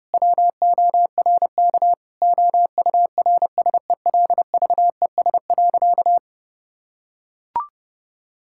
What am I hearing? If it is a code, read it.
Morse code, heard as a keyed tone, about 32 wpm, 700 Hz. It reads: WORK OURSELVES.
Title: Sentences from Top 1600 Words 30wpm